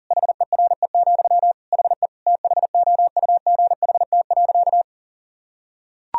Morse code, heard as a keyed tone, about 40 wpm, 700 Hz. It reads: HERE, HE THOUGHT.
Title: Sentences from Top 1000 Words - Review Part 3 - 40wpm